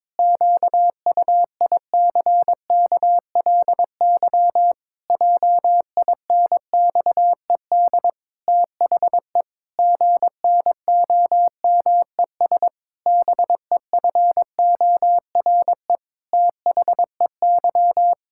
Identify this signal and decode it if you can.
Morse code, heard as a keyed tone, about 22 wpm, 700 Hz. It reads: QUICKLY JINXED THE GNOMES BEFORE THEY
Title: Pangram Challenge v2 - 22wpm